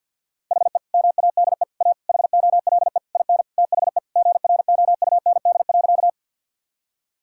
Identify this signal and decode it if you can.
Morse code, heard as a keyed tone, about 52 wpm, 700 Hz. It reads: HE MADE A HOLE IN THE GROUND.
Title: Sentences from Top 1600 Words 50wpm